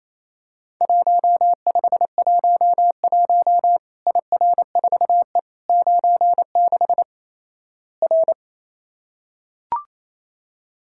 Morse code, heard as a keyed tone, about 28 wpm, 700 Hz. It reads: 1511 IR4E 96 R
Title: Worked All Europe DX Contest - QTC Exchange - 28wpm